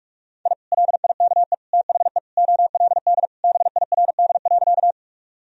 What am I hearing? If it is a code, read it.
Morse code, heard as a keyed tone, about 46 wpm, 700 Hz. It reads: I LIKE THE OLD BIRD.
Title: Sentences from Top 1000 Words - Review Part 2 - 45wpm